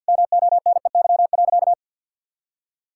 Morse code, heard as a keyed tone, about 50 wpm, 700 Hz. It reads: MONEY.
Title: Sentences from Top 1600 Words 50wpm